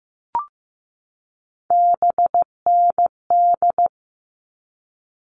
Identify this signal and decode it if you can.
Morse code, heard as a keyed tone, about 15 wpm, 700 Hz. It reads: BND
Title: All Cut Numbers - 3-Digit 15wpm